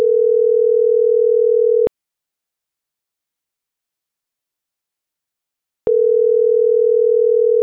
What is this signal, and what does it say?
Ringback tone (North America), 440 + 480 Hz.